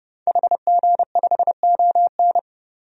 Morse code, heard as a keyed tone, about 30 wpm, 700 Hz. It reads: HG5ON
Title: DXCC-100 Call Signs - 5 Characters - 30wpm